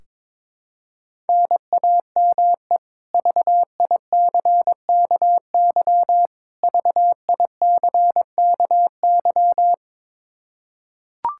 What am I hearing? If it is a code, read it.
Morse code, heard as a keyed tone, about 22 wpm, 700 Hz. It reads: NAME VICKY VICKY
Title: K1USN SST QSO Practice 22wpm